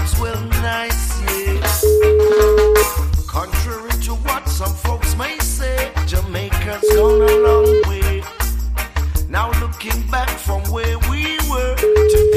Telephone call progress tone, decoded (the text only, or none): ringback tone